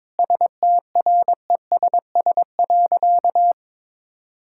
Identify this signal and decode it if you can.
Morse code, heard as a keyed tone, about 22 wpm, 700 Hz. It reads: STRESS.
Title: Sentences from Top 1000 Words - Review Part 6 - 22wpm